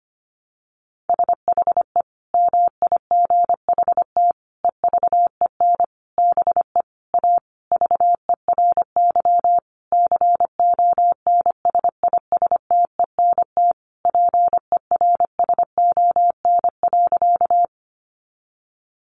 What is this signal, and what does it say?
Morse code, heard as a keyed tone, about 25 wpm, 700 Hz. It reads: SHE MIGHT EVEN BE A VERY CONSISTENT PERSON.
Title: Sentences from Top 1500 Words 25wpm